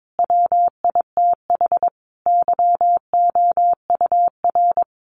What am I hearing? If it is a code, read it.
Morse code, heard as a keyed tone, about 22 wpm, 700 Hz. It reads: WITH YOUR
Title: Sentences from Top 1500 Words 22wpm